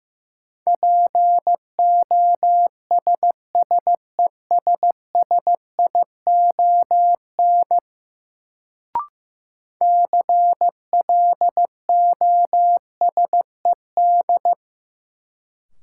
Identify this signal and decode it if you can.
Morse code, heard as a keyed tone, about 15 wpm, 700 Hz. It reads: POSSESSION CLOSED
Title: Sentences from Top 1400 Words - New Word Drill - 15wpm